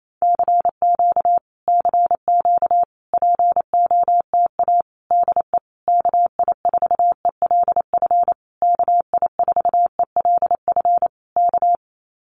Morse code, heard as a keyed tone, about 28 wpm, 700 Hz. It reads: CQ CQ POTA DE KI4ELF KI4ELF K
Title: Common QSO Phrases Encore 28wpm